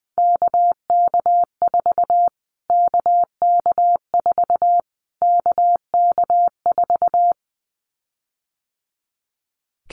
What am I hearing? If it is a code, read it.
Morse code, heard as a keyed tone, about 20 wpm, 700 Hz. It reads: KK4 KK4 KK4
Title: Callsign Prefixes for Top 25 DXCC - Repeated 3x - 20wpm